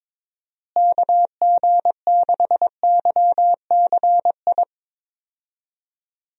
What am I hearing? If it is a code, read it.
Morse code, heard as a keyed tone, about 22 wpm, 700 Hz. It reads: KG6YCI
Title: USA Call Signs - 6 Characters - Repeated 3x - 22wpm